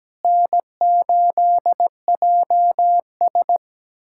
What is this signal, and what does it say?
Morse code, heard as a keyed tone, about 17 wpm, 700 Hz. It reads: N8JS
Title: K1USN SST QSO Practice - 17wpm